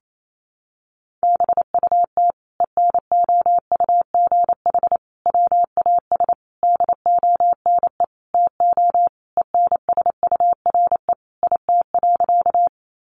Morse code, heard as a keyed tone, about 28 wpm, 700 Hz. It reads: BUT ENOUGH WAS DONE TO ENSURE IT.
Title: Sentences from Top 1100 Words 28wpm